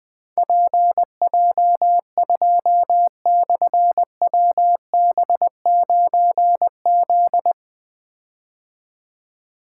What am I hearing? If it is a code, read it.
Morse code, heard as a keyed tone, about 20 wpm, 700 Hz. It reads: PJ2/WB9Z
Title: Difficult Call Signs 20wpm